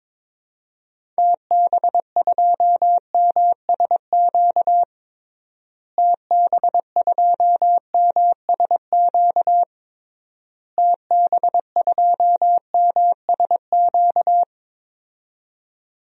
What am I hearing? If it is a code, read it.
Morse code, heard as a keyed tone, about 22 wpm, 700 Hz. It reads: TB2MSQ TB2MSQ TB2MSQ
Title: DXCC-100 Call Signs - 6 Characters - Repeated 3x - 22wpm